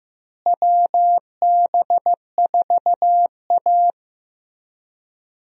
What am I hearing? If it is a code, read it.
Morse code, heard as a keyed tone, about 16 wpm, 700 Hz. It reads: WB4A
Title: USA Call Signs - 4 Characters - Repeated 3x - 15wpm